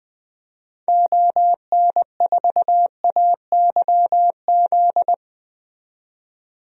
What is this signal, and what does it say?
Morse code, heard as a keyed tone, about 20 wpm, 700 Hz. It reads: ON4AYZ
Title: DXCC-100 Call Signs - 6 Characters - 20wpm